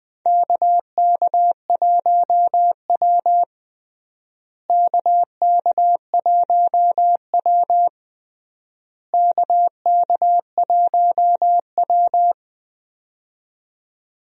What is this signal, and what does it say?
Morse code, heard as a keyed tone, about 20 wpm, 700 Hz. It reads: KK1W KK1W KK1W
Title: USA Call Signs - 4 Characters - Repeated 3x - 20wpm